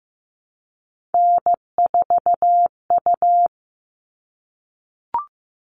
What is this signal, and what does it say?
Morse code, heard as a keyed tone, about 15 wpm, 700 Hz. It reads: N4U
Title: All Cut Numbers - 3-Digit 15wpm